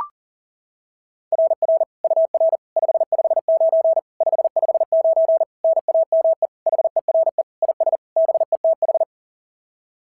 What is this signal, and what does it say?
Morse code, heard as a keyed tone, about 40 wpm, 690 Hz. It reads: RR UR 559 559 NAME HERE IS BETH
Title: SOTA QSO Practice 40wpm